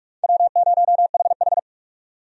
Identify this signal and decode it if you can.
Morse code, heard as a keyed tone, about 46 wpm, 700 Hz. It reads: W0HH
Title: USA Call Signs - 4 Characters - 45wpm